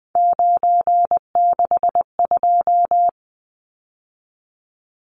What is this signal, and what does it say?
Morse code, heard as a keyed tone, about 20 wpm, 700 Hz. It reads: 962
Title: Numbers - 3-Digit 20wpm